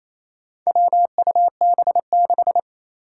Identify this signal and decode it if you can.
Morse code, heard as a keyed tone, about 28 wpm, 700 Hz. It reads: WUB6
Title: Four Challenging Letter-Number Combinations 28wpm